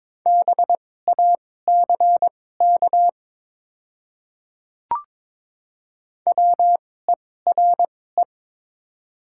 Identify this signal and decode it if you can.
Morse code, heard as a keyed tone, about 22 wpm, spaced as for 16 wpm, 700 Hz. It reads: BACK WERE
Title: 2x Character Spacing - Top 200 Words - 22wpm